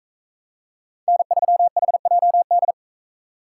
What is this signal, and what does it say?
Morse code, heard as a keyed tone, about 42 wpm, 700 Hz. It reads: N3HJD